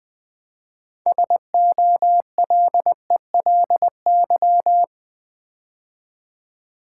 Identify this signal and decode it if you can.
Morse code, heard as a keyed tone, about 20 wpm, 700 Hz. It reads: SOLELY
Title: 6 Letter Words - First 3 Letters Given - 20wpm